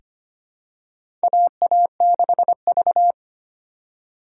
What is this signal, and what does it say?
Morse code, heard as a keyed tone, about 25 wpm, 700 Hz. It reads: AA6V
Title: USA Call Signs - 4x Character Spacing - 25wpm